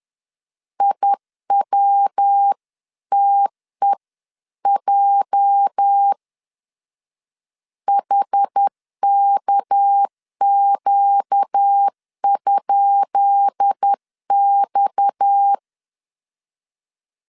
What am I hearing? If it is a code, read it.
Morse code, heard as a keyed tone, about 10 wpm, 800 Hz. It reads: IWTEJ HKQ?X
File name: LEZIONI / 10 SIGNS / 10B INT 10wpm 2844